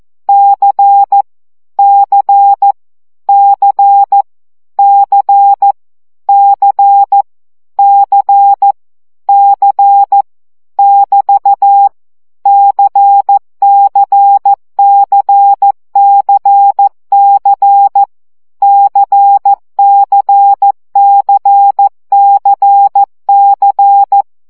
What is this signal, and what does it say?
Morse code, heard as a keyed tone, about 14 wpm, 800 Hz. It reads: C C C C C C C <BT> CCCCC CCCCC
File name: CW gyakorlas c 12 baud